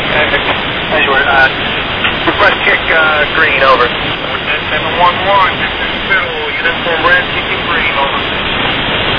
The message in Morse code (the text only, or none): none